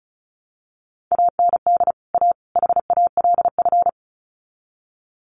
Morse code, heard as a keyed tone, about 34 wpm, 700 Hz. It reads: AND A HALF
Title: Sets of 3 Words 35wpm